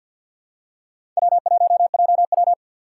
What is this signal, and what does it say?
Morse code, heard as a keyed tone, about 53 wpm, 700 Hz. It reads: W1JW
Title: USA Call Signs - 4 Characters - 2x Character Spacing - 50wpm